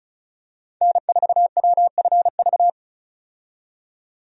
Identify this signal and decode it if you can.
Morse code, heard as a keyed tone, about 36 wpm, 700 Hz. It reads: N4WFV
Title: USA Call Signs - 5 Characters - 35wpm